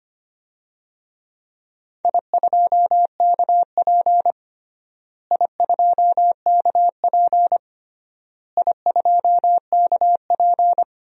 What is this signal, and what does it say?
Morse code, heard as a keyed tone, about 25 wpm, 700 Hz. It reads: I2KP I2KP I2KP
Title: DXCC-100 Call Signs - 4 Characters - Repeated 3x - 25wpm